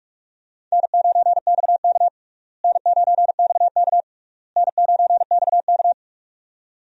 Morse code, heard as a keyed tone, about 42 wpm, 700 Hz. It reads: N9XK N9XK N9XK
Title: USA Call Signs - 4 Characters - Repeated 3x - 45wpm